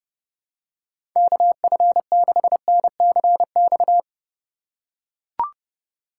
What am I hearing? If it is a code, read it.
Morse code, heard as a keyed tone, about 29 wpm, 700 Hz. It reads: KF6NCX
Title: K1USN SST QSO Practice - 30wpm